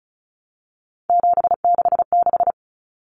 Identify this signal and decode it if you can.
Morse code, heard as a keyed tone, about 34 wpm, 700 Hz. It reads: 766
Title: Common Cut Numbers - 3-Digit 35wpm